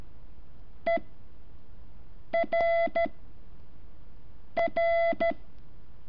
Morse code, heard as a keyed tone, about 13 wpm, 670 Hz. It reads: E R R
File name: curso / ejercicio 11